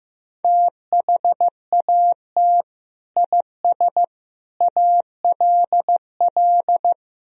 Morse code, heard as a keyed tone, about 15 wpm, 700 Hz. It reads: THAT IS ALL
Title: Sentences from Top 200 Words and Abbreviations 15wpm